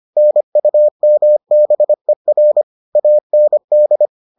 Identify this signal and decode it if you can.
Morse code, heard as a keyed tone, about 25 wpm, 600 Hz. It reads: NUMBER AND